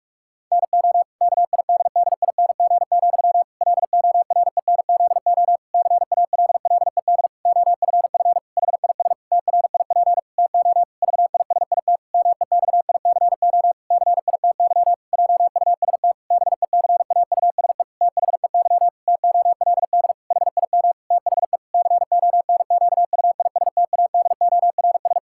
Morse code, heard as a keyed tone, about 41 wpm, 700 Hz. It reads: NO KIDDING, LORENZO CALLED OFF HIS TRIP TO VISIT MEXICO CITY JUST BECAUSE THEY TOLD HIM THE CONQUISTADORS
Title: Pangram Challenge v2 - 45wpm